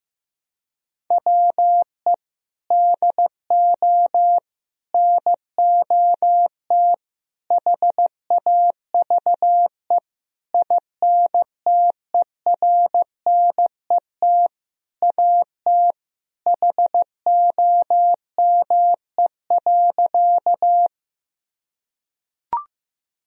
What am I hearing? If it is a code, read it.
Morse code, heard as a keyed tone, about 15 wpm, 700 Hz. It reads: WE DO NOT HAVE INTERNET AT HOME.
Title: Sentences from Top 1000 Words - Review Part 4 - 15wpm